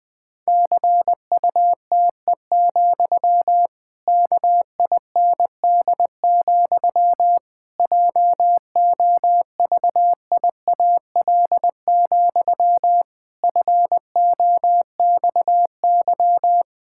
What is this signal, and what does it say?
Morse code, heard as a keyed tone, about 20 wpm, 700 Hz. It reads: CUTE, KIND, JOVIAL, FOXY
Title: Pangram Challenge v2 - 20wpm